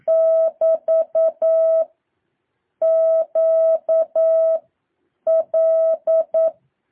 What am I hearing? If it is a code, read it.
Morse code, heard as a keyed tone, about 9 wpm, 650 Hz. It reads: <BT> Q L
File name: jh8cbh / 2023cw